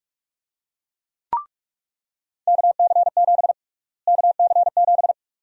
Morse code, heard as a keyed tone, about 41 wpm, 700 Hz. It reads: KC7 KC7
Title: USA Call Sign Prefixes - Repeated 2x - 45wpm